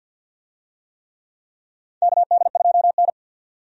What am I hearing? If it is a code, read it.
Morse code, heard as a keyed tone, about 53 wpm, 700 Hz. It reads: KD2N
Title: USA Call Signs - 4 Characters - 50wpm